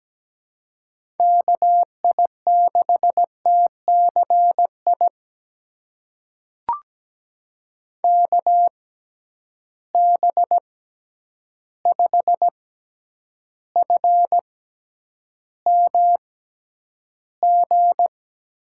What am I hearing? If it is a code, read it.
Morse code, heard as a keyed tone, about 17 wpm, spaced as for 9 wpm, 700 Hz. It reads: KI6TCI KB5FMG
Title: USA Call Signs - 6x Character Spacing - 17wpm